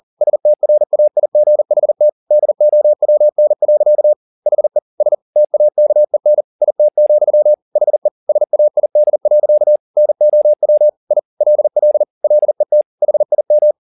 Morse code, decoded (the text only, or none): STRAIGHT DOWN. HE S TAKEN IT, HE SAID. NOW I LL LET HIM